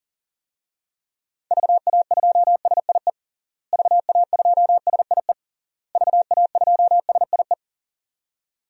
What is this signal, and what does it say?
Morse code, heard as a keyed tone, about 40 wpm, 700 Hz. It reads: VA2SIE VA2SIE VA2SIE
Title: DXCC-100 Call Signs - 6 Characters - Repeated 3x - 40wpm